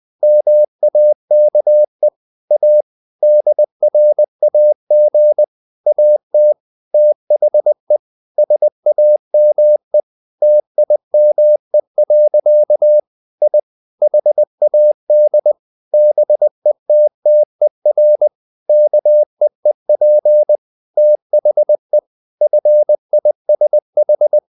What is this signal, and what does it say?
Morse code, heard as a keyed tone, about 20 wpm, 600 Hz. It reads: MAKE A DRAG AT THE SAME TIME. I HAD BETTER KEEP THE FISH
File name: the old man and the sea part 2 20wpm